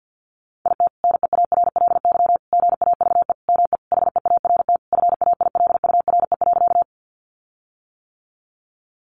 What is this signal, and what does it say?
Morse code, heard as a keyed tone, about 50 wpm, 700 Hz. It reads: IT NEARLY GAVE ME HEART FAILURE.
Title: Sentences from Top 1300 Words 50wpm